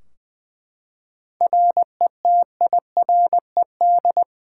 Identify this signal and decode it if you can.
Morse code, heard as a keyed tone, about 20 wpm, 700 Hz. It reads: RETIRED
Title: Practice QSO - Exchange 3 - More Info - 20wpm